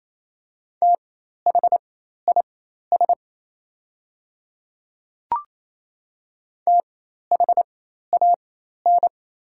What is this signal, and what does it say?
Morse code, heard as a keyed tone, about 28 wpm, spaced as for 13 wpm, 690 Hz. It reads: THIS THAN
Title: 4x Character Spacing - Top 100 Words - 28wpm